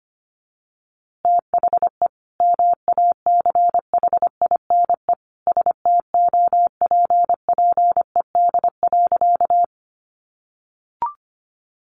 Morse code, heard as a keyed tone, about 25 wpm, 700 Hz. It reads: THE MACHINE STOPPED.
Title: A Sound Of Thunder 25wpm